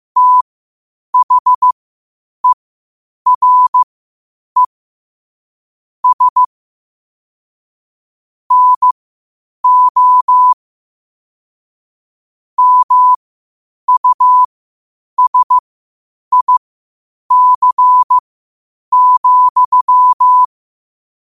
Morse code, heard as a keyed tone, about 15 wpm, spaced as for 8 wpm, 1000 Hz. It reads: THERE S NO MUSIC,
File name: qotdfiles / 8wpm